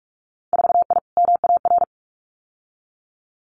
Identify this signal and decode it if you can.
Morse code, heard as a keyed tone, about 46 wpm, 700 Hz. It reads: 4I MAR
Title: Winter Field Day Exchanges Encore 45wpm